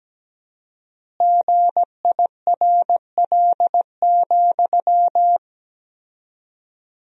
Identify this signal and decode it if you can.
Morse code, heard as a keyed tone, about 17 wpm, 700 Hz. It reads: GIRL,
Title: 4x Word-Spacing - Sentences from Top 400 Words - 17wpm